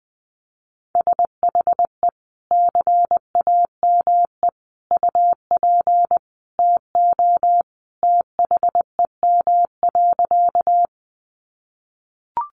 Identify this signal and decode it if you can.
Morse code, heard as a keyed tone, about 20 wpm, 700 Hz. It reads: SHE CAME UP TO THEM.
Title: Sentences from Top 200 Words 20wpm